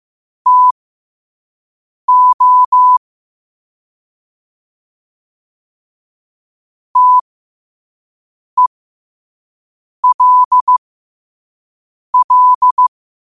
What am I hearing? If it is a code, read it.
Morse code, heard as a keyed tone, about 15 wpm, spaced as for 5 wpm, 1000 Hz. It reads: TO TELL
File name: qotdfiles / 5wpm